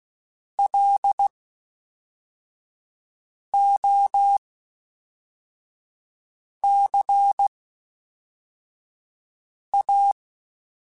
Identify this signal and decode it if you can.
Morse code, heard as a keyed tone, about 16 wpm, 800 Hz. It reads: L O C A
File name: Special INGs / Special INGs-mp3s / Fast ing @10